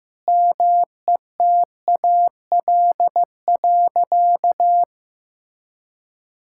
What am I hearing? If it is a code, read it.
Morse code, heard as a keyed tone, about 16 wpm, 700 Hz. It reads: METAL.
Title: Sentences from Top 1600 Words 15wpm